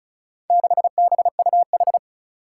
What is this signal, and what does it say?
Morse code, heard as a keyed tone, about 36 wpm, 700 Hz. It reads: 6BUH
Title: Four Challenging Letter-Number Combinations 35wpm